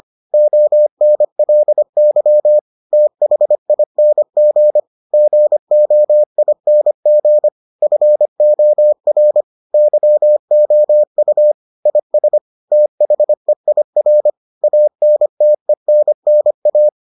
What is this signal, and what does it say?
Morse code, heard as a keyed tone, about 25 wpm, 600 Hz. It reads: ONLY THING GOING FOR YOU IS THEIR ANTENNA